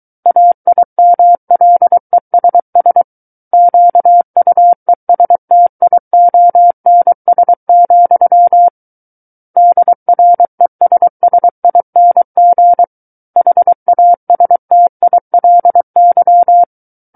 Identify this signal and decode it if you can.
Morse code, heard as a keyed tone, about 23 wpm, 700 Hz. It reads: AIMLESS QUESTIONS, DRESSING HASTILY
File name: Book One / 14-Chapter 14 - 23 WPM